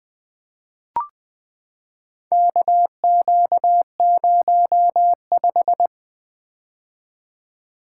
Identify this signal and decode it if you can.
Morse code, heard as a keyed tone, about 20 wpm, 700 Hz. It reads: KQ05
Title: Four Letter-Number Combinations - 20wpm